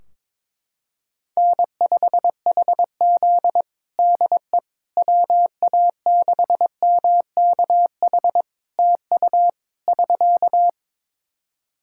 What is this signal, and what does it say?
Morse code, heard as a keyed tone, about 22 wpm, 700 Hz. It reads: N5HZ DE WA6MKH TU <SK>
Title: Practice QSO - Exchange 4 - Ending - 22wpm